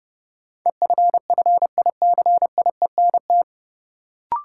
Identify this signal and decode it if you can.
Morse code, heard as a keyed tone, about 30 wpm, 700 Hz. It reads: EFFICIENT
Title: Sentences from Top 1300 Words - New Word Drill - 30wpm